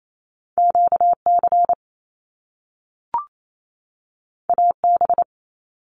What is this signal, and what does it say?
Morse code, heard as a keyed tone, about 28 wpm, 700 Hz. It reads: QC AB
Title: Canadian Provinces and Territories 28wpm